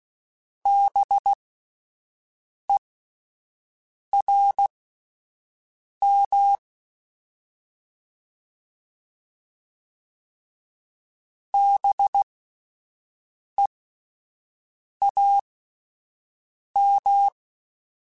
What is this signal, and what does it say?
Morse code, heard as a keyed tone, about 16 wpm, spaced as for 5 wpm, 800 Hz. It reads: BERM BEAM
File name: Spaced Alphas / Spaced Alphas mp3s / Bs sp @10